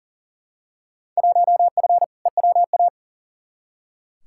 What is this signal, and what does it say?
Morse code, heard as a keyed tone, about 40 wpm, 700 Hz. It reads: 1F EWA
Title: Field Day Exchanges Encore - 40wpm